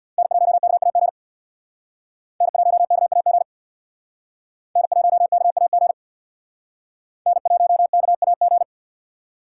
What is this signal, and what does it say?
Morse code, heard as a keyed tone, about 63 wpm, 700 Hz. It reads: N1KAG N1KAG N1KAG N1KAG